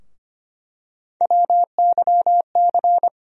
Morse code, heard as a keyed tone, about 25 wpm, 700 Hz. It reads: WYC
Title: USA Call Sign Postfixes - 3 Letters 25wpm